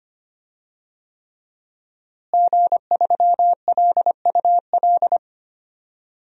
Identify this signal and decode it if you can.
Morse code, heard as a keyed tone, about 25 wpm, 700 Hz. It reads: G3LUL
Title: DXCC-100 Call Signs - 5 Characters - Repeated 3x - 25wpm